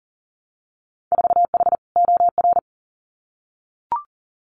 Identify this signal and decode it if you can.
Morse code, heard as a keyed tone, about 40 wpm, 700 Hz. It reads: 4H OR
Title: Winter Field Day Exchanges Encore 40wpm